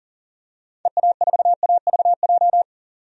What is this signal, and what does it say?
Morse code, heard as a keyed tone, about 40 wpm, 700 Hz. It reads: EA4AVJ
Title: DXCC-100 Call Signs - 6 Characters - 40wpm